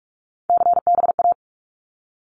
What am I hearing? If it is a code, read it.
Morse code, heard as a keyed tone, about 46 wpm, 700 Hz. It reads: CBA
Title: Abbreviations 45wpm